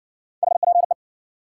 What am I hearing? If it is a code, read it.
Morse code, heard as a keyed tone, about 58 wpm, 700 Hz. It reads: HPE